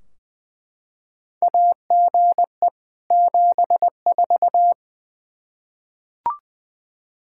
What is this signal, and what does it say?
Morse code, heard as a keyed tone, about 20 wpm, 700 Hz. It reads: AGE 74
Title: Practice QSO - Exchange 3 - More Info - 20wpm